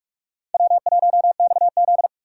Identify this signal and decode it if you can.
Morse code, heard as a keyed tone, about 44 wpm, 700 Hz. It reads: W1XZ